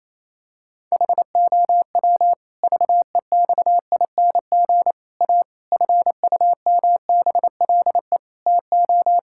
Morse code, heard as a keyed tone, about 27 wpm, 700 Hz. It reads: HOW VEXING A FUMBLE TO
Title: Pangram Challenge v2 - 28wpm